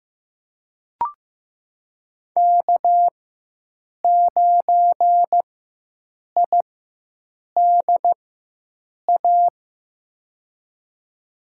Morse code, heard as a keyed tone, about 15 wpm, spaced as for 7 wpm, 700 Hz. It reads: K9IDA
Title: USA Call Signs - 5 Characters - 4x Character Spacing - 15wpm